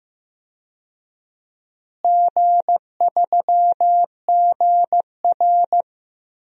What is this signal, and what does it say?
Morse code, heard as a keyed tone, about 15 wpm, 700 Hz. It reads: G3GR
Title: DXCC-100 Call Signs - 4 Characters - Repeated 2x - 15wpm